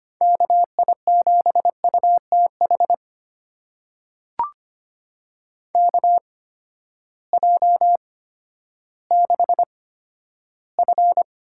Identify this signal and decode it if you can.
Morse code, heard as a keyed tone, about 25 wpm, spaced as for 12 wpm, 700 Hz. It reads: KI7UTH KJ6F
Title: USA Call Signs - 6 Characters - 8x Character Spacing - 25wpm